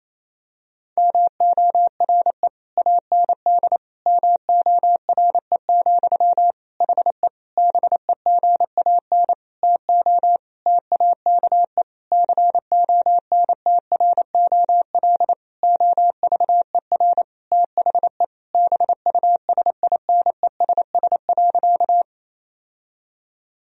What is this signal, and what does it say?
Morse code, heard as a keyed tone, about 28 wpm, 700 Hz. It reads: MORE AND MORE, HE BEGAN TO TAKE CONTROL OVER THE BUSINESS.
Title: Sentences from Top 1000 Words - Review Part 3 - 28wpm